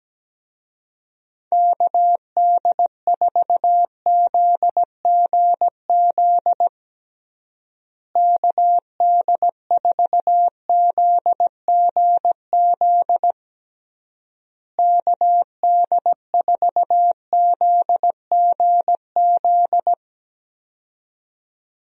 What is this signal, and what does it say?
Morse code, heard as a keyed tone, about 17 wpm, 700 Hz. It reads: KD4ZGZ KD4ZGZ KD4ZGZ
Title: USA Call Signs - 6 Characters - Repeated 3x - 17wpm